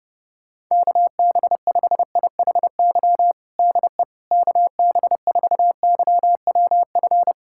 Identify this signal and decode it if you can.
Morse code, heard as a keyed tone, about 30 wpm, 700 Hz. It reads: KB5IHY DE KB4YWF
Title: USA Callsign DE Callsign - 6 Characters - 30wpm